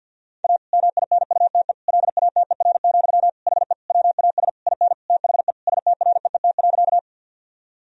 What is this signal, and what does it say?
Morse code, heard as a keyed tone, about 50 wpm, 700 Hz. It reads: A MINUTE LATER, HE WAS IN THE STREET.
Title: Sentences from Top 1000 Words - Review Part 4 - 50wpm